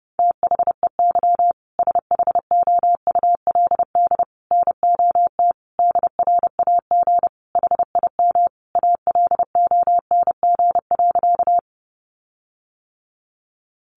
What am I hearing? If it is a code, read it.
Morse code, heard as a keyed tone, about 30 wpm, 700 Hz. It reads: THEY SHOULD NOT DRAG HIM ALONG.
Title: Sentences from Top 1500 Words 30wpm